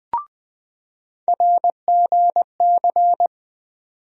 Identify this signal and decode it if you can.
Morse code, heard as a keyed tone, about 21 wpm, 700 Hz. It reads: RGC
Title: USA Call Sign Postfixes - 3 Letters 20wpm